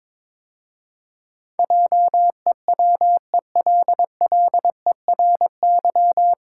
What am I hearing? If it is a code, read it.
Morse code, heard as a keyed tone, about 22 wpm, 700 Hz. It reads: JEWELLERY
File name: 9 Letter Words - First 3 Letters Given - 22wpm